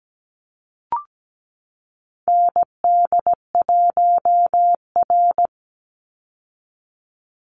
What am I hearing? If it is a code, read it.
Morse code, heard as a keyed tone, about 17 wpm, 700 Hz. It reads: ND1R
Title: Call Signs Repeated 3x - 17wpm